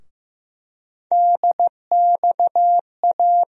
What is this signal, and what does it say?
Morse code, heard as a keyed tone, about 15 wpm, 700 Hz. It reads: DXA
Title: USA Call Sign Postfixes - 3 Letters 15wpm